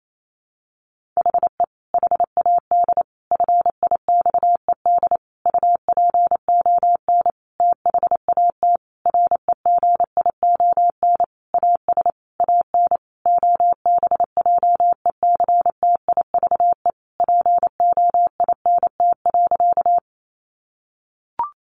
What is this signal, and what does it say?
Morse code, heard as a keyed tone, about 28 wpm, 700 Hz. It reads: HE HAD FIXED UPON THAT REGION AS AN OBJECTIVE POINT.
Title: Sentences from Top 1500 Words 28wpm